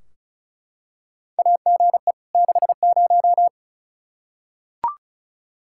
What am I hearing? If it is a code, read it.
Morse code, heard as a keyed tone, about 34 wpm, 700 Hz. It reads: AGE 60
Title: Practice QSO - Exchange 3 - More Info - 35wpm